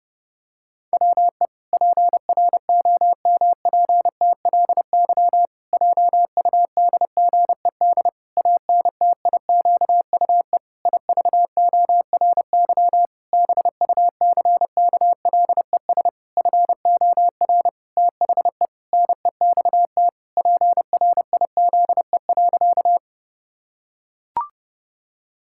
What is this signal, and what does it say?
Morse code, heard as a keyed tone, about 30 wpm, 700 Hz. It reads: WE PROMPTLY JUDGED ANTIQUE IVORY BUCKLES FOR THE NEXT PRIZE.
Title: Pangram Challenge v2 - 30wpm